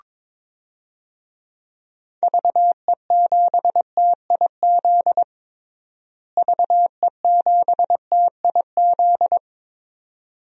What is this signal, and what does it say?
Morse code, heard as a keyed tone, about 22 wpm, 700 Hz. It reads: VE7TIZ VE7TIZ
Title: DXCC-100 Call Signs - 6 Characters - Repeated 2x - 22wpm